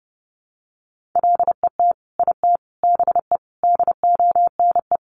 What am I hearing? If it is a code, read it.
Morse code, heard as a keyed tone, about 28 wpm, 700 Hz. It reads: LET IT BE DONE
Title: Sentences from Top 1100 Words 30wpm